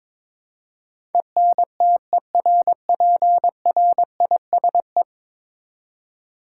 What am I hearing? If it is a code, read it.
Morse code, heard as a keyed tone, about 21 wpm, 700 Hz. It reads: ENTERPRISE
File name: practice / 22wpm / Ten Letter Words - 22wpm